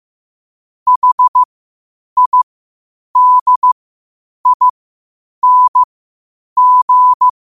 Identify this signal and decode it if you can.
Morse code, heard as a keyed tone, about 15 wpm, spaced as for 8 wpm, 1000 Hz. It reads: HIDING
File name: qotdfiles / 8wpm